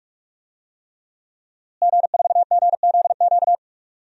Morse code, heard as a keyed tone, about 43 wpm, 700 Hz. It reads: G4GZQ
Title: DXCC-100 Call Signs - 5 Characters - Repeated 3x - 45wpm